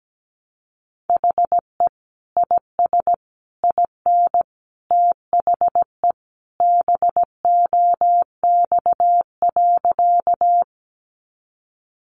Morse code, heard as a keyed tone, about 17 wpm, 700 Hz. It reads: HE IS IN THE BOX.
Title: Sentences from Top 500 Words 17wpm